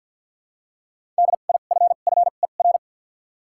Morse code, heard as a keyed tone, about 48 wpm, 700 Hz. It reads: D I F F E R
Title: 2x Character Spacing - Top 500 Words - 50wpm